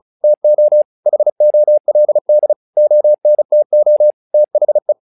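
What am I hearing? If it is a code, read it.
Morse code, heard as a keyed tone, about 34 wpm, 600 Hz. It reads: TO HOLD ONTO THE
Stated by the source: Morse practice text "Tower Trouble" sent at 35 WPM, from the SKCC - Straight Key Century Club learning center audio files.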